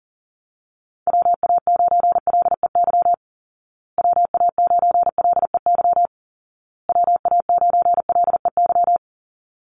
Morse code, heard as a keyed tone, about 40 wpm, 700 Hz. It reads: WA9LEY WA9LEY WA9LEY
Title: Call Signs Repeated 2x - 40wpm